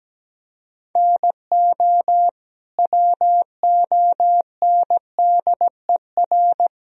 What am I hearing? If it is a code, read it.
Morse code, heard as a keyed tone, about 17 wpm, 700 Hz. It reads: NO WONDER
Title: Sentences from Top 1000 Words - Review Part 3 - 17wpm